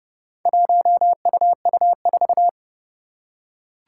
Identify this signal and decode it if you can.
Morse code, heard as a keyed tone, about 31 wpm, 700 Hz. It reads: 1UU4
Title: Four Challenging Letter-Number Combinations 30wpm